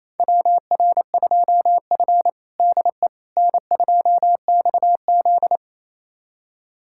Morse code, heard as a keyed tone, about 27 wpm, 700 Hz. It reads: WR2F DE N2XZ
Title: USA Callsign DE Callsign - 4 Characters - 28wpm